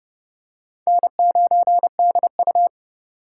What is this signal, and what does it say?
Morse code, heard as a keyed tone, about 30 wpm, 700 Hz. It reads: N9DU
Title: USA Call Signs - 4 Characters - 4x Character Spacing - 30wpm